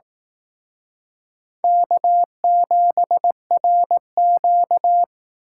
Morse code, heard as a keyed tone, about 18 wpm, 700 Hz. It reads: K7RQ
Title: Speed Racing - USA 1x2 Call Signs 15wpm